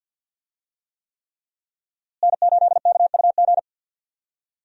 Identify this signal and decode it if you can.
Morse code, heard as a keyed tone, about 48 wpm, 700 Hz. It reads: N8KUG
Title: USA Call Signs - 5 Characters - Repeated 3x - 50wpm